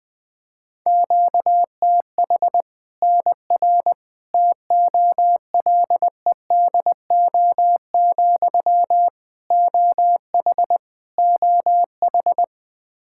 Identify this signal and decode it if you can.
Morse code, heard as a keyed tone, about 20 wpm, 700 Hz. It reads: QTH NR TOLEDO, OH OH
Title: Practice QSO - Exchange 1 - Beginning - 20wpm